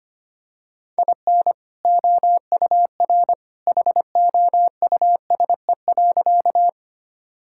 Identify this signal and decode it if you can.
Morse code, heard as a keyed tone, about 25 wpm, 700 Hz. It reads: IN OUR HOUSE.
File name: practice / 25wpm / Sentences with 2-Word Phrases - 25wpm